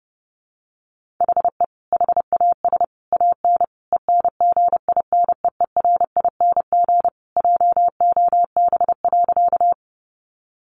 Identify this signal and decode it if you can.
Morse code, heard as a keyed tone, about 30 wpm, 700 Hz. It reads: HE HAS AN ENGINEERING JOB.
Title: Sentences from Top 1500 Words 30wpm